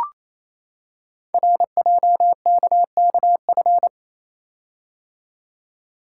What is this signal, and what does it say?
Morse code, heard as a keyed tone, about 29 wpm, 700 Hz. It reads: RJKKF
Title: Code Groups - 5 Letters - 28wpm